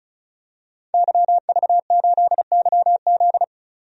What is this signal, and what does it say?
Morse code, heard as a keyed tone, about 36 wpm, 700 Hz. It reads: YV8YZ
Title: DXCC-100 Call Signs - 5 Characters - 35wpm